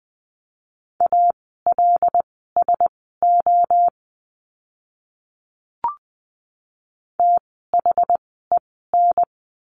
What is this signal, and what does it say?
Morse code, heard as a keyed tone, about 20 wpm, spaced as for 15 wpm, 700 Hz. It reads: ALSO THEN
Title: 101 Lesson 17f - Letter u - Words 20wpm